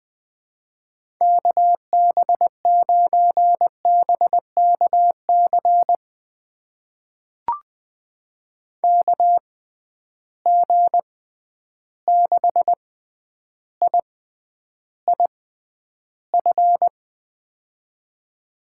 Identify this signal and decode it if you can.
Morse code, heard as a keyed tone, about 21 wpm, spaced as for 10 wpm, 700 Hz. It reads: KB9BKC KG6IIF
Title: USA Call Signs - 6 Characters - 6x Character Spacing - 20wpm